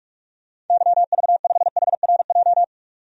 Morse code, heard as a keyed tone, about 42 wpm, 700 Hz. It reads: YV5HRJ
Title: DXCC-100 Call Signs - 6 Characters - 45wpm